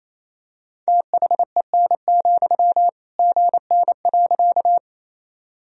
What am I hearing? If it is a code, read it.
Morse code, heard as a keyed tone, about 28 wpm, 700 Hz. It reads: THEN, GN.
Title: Sentences from Top 300 Words and Abbreviations 28wpm